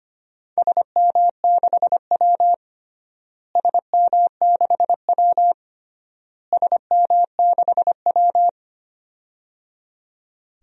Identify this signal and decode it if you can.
Morse code, heard as a keyed tone, about 25 wpm, 700 Hz. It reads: SM6W SM6W SM6W
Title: DXCC-100 Call Signs - 4 Characters - Repeated 3x - 25wpm